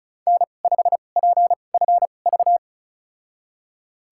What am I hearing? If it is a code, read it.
Morse code, heard as a keyed tone, about 34 wpm, spaced as for 25 wpm, 700 Hz. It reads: N5PFV
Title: USA Call Signs - 2x Character Spacing - 35wpm